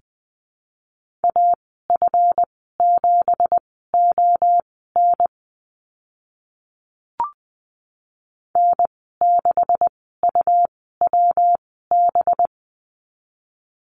Morse code, heard as a keyed tone, about 20 wpm, spaced as for 14 wpm, 700 Hz. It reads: AF7ON N6UWB
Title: 193 Lesson 31g - Letter v - Call Signs 20wpm